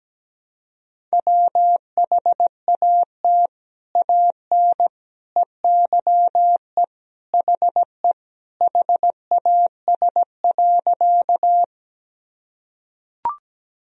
Morse code, heard as a keyed tone, about 17 wpm, 700 Hz. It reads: WHAT AN EYE HE HAS.
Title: Sentences from Top 1000 Words - Review Part 1 - 17wpm